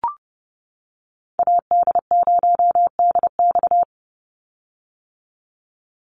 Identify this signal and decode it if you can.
Morse code, heard as a keyed tone, about 30 wpm, 700 Hz. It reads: AD0DX
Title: Call Signs - 30wpm